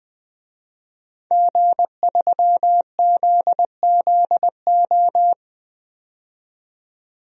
Morse code, heard as a keyed tone, about 20 wpm, 700 Hz. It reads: G3ZZO
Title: DXCC-100 Call Signs - 5 Characters - 20wpm